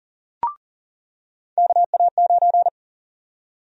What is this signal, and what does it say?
Morse code, heard as a keyed tone, about 40 wpm, 700 Hz. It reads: KA9
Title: USA Call Sign Prefixes - 40wpm